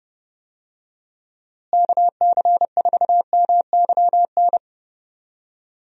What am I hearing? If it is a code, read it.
Morse code, heard as a keyed tone, about 30 wpm, 700 Hz. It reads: KC4MYN
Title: USA Call Signs - 6 Characters - Repeated 2x - 30wpm